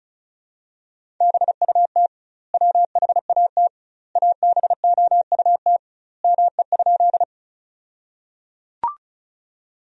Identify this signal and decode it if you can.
Morse code, heard as a keyed tone, about 34 wpm, 700 Hz. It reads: BUT WHAT ABOUT ME?
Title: 2x Word-Spacing - Sentences from Top 200 Words - 35wpm